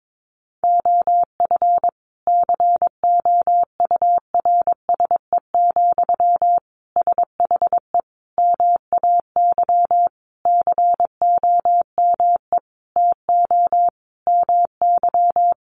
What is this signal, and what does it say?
Morse code, heard as a keyed tone, about 22 wpm, 700 Hz. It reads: OF COURSE, SHE MAY COME TO MY
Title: Sentences from Top 1400 Words 22wpm